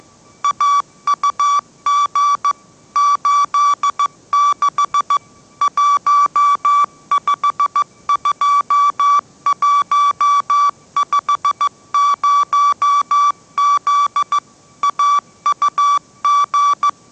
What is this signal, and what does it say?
Morse code, heard as a keyed tone, about 15 wpm, 1150 Hz. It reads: AUG 86 152150Z AUG
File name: mrx / mrx065 16wpm